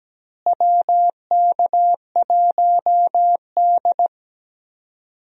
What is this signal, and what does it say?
Morse code, heard as a keyed tone, about 17 wpm, 700 Hz. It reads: WK1D
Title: USA Call Signs - 4 Characters - Repeated 2x - 17wpm